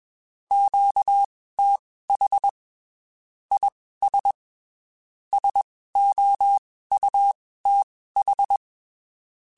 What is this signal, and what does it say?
Morse code, heard as a keyed tone, about 21 wpm, spaced as for 15 wpm, 800 Hz. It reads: QTH IS SOUTH
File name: (spaced) / QSOs Spaced-mp3s / kc7sac @20 sp